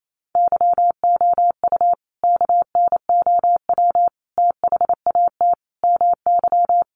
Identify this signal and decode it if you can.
Morse code, heard as a keyed tone, about 28 wpm, 700 Hz. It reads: YOU KNOW THAT MY
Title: Sentences from Top 1400 Words 28wpm